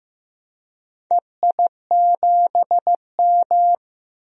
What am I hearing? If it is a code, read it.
Morse code, heard as a keyed tone, about 15 wpm, 700 Hz. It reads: EI7M
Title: DXCC-100 Call Signs - 4 Characters - Repeated 2x - 15wpm